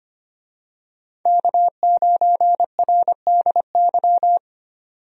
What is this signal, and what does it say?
Morse code, heard as a keyed tone, about 25 wpm, 700 Hz. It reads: K9RDY
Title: USA Call Signs - 5 Characters - 25wpm